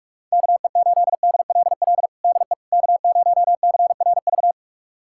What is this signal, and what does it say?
Morse code, heard as a keyed tone, about 46 wpm, 700 Hz. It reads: KE8DLL DE K0CRV
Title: USA Callsign DE Callsign 45wpm